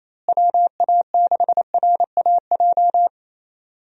Morse code, heard as a keyed tone, about 27 wpm, 700 Hz. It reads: WA6RAJ
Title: USA Call Signs - 6 Characters - Repeated 2x - 28wpm